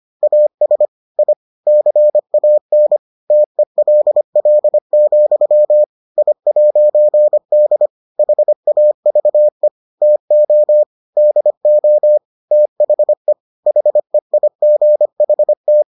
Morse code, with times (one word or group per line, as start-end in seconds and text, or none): 0.2-0.9 AS
1.2-1.3 I
1.7-3.0 CAN
3.3-5.8 TELL,
6.2-7.9 I'D
8.2-9.7 HAVE
10.0-10.8 TO
11.2-12.2 DO
12.5-13.3 THE
13.7-15.8 HEIGHT